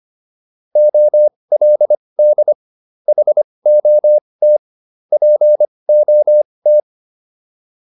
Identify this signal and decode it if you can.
Morse code, heard as a keyed tone, about 25 wpm, spaced as for 20 wpm, 600 Hz. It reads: OLD HOT POT